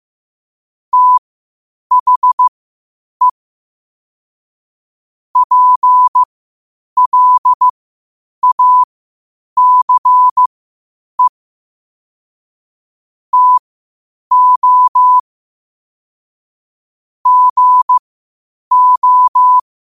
Morse code, heard as a keyed tone, about 15 wpm, spaced as for 8 wpm, 1000 Hz. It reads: THE PLACE TO GO
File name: qotdfiles / 8wpm